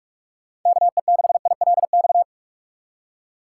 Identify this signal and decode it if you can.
Morse code, heard as a keyed tone, about 46 wpm, 700 Hz. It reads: KE6ILX
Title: USA Call Signs - 6 Characters - 6x Character Spacing - 45wpm